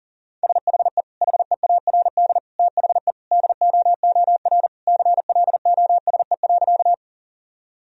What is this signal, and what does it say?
Morse code, heard as a keyed tone, about 40 wpm, 700 Hz. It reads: SHE HEARD THE DOOR CLOSE.
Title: Sentences from Top 1000 Words - Review Part 1 - 40wpm